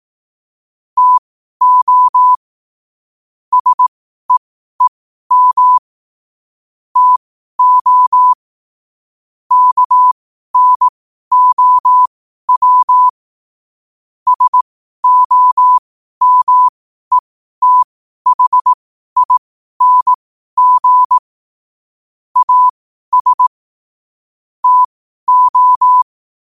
Morse code, heard as a keyed tone, about 18 wpm, spaced as for 12 wpm, 1000 Hz. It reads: TO SEEM TO KNOW SOMETHING AS TO
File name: qotdfiles / 12wpm